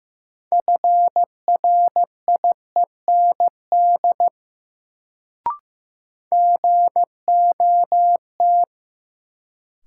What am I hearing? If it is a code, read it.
Morse code, heard as a keyed tone, about 15 wpm, 700 Hz. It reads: FRIEND GOT
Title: Sentences from Top 300 Words - New Word Drill - 15wpm